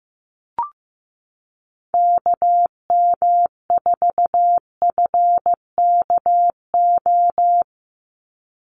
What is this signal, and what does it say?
Morse code, heard as a keyed tone, about 15 wpm, 700 Hz. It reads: KM4FKO
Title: USA Call Signs 15wpm